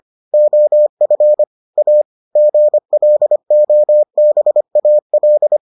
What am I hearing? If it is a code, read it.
Morse code, heard as a keyed tone, about 25 wpm, 600 Hz. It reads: OF A GLOBAL